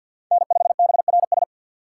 Morse code, heard as a keyed tone, about 50 wpm, 700 Hz. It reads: N5BRS
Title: USA Call Signs - 3x Character Spacing - 50wpm